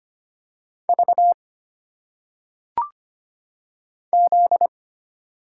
Morse code, heard as a keyed tone, about 25 wpm, 700 Hz. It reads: V Z
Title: Single Letters - Phonetic Pronunciation 25wpm